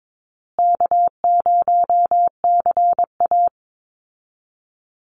Morse code, heard as a keyed tone, about 22 wpm, 700 Hz. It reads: K0CA
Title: Call Signs Repeated 2x - 22wpm